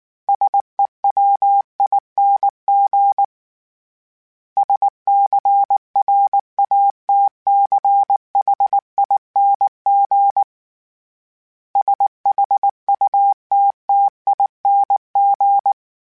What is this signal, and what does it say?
Morse code, heard as a keyed tone, about 19 wpm, 800 Hz. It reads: SEWING SCRATCHING SHUTTING
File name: Special INGs / Special INGs-mp3s / Sngl Ing @18